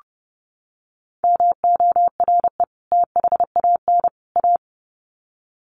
Morse code, heard as a keyed tone, about 30 wpm, 700 Hz. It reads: MORE THAN A
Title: Sets of 3 Words 30wpm